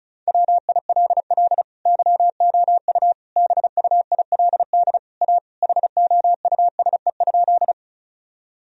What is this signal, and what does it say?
Morse code, heard as a keyed tone, about 34 wpm, 700 Hz. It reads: WILL YOU BUILD A HOUSE?
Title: Sentences from Top 1000 Words - Review Part 1 - 35wpm